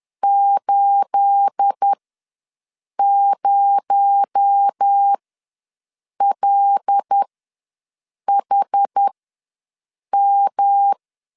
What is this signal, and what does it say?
Morse code, heard as a keyed tone, about 10 wpm, spaced as for 6 wpm, 800 Hz. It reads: 80LHM
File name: LEZIONI / 9 28 ws 10wpm 4124